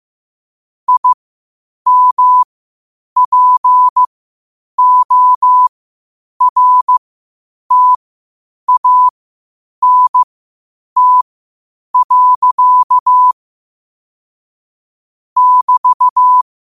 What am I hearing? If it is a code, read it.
Morse code, heard as a keyed tone, about 15 wpm, spaced as for 8 wpm, 1000 Hz. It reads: IMPORTANT. <BT>
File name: qotdfiles / 8wpm